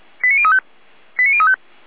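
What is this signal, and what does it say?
Five-tone selective call (ZVEI), heard as quick runs of five tones, tones of about 70 ms. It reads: 89035 89035
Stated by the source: ZVEI five-tone selective call 89006 received by SDR.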